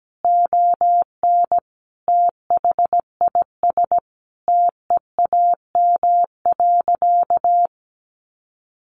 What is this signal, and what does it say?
Morse code, heard as a keyed tone, about 17 wpm, 700 Hz. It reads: ON THIS TEAM.
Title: Sentences from Top 1500 Words 17wpm